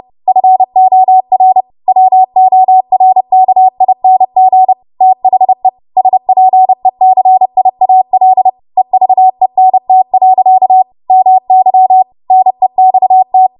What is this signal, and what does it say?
Morse code, heard as a keyed tone, about 30 wpm, 750 Hz. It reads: FOR WORKING THE SPECIAL EVENT. MY NEXT
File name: Morse / 30WPM